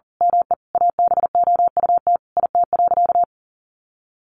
Morse code, heard as a keyed tone, about 40 wpm, 700 Hz. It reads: ME ABOUT IT.
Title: Sentences from Top 200 Words 40wpm